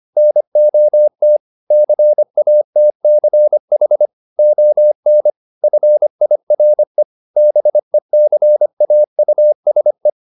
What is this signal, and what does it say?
Morse code, heard as a keyed tone, about 25 wpm, 600 Hz. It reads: NOT CATCH ON FIRE BECAUSE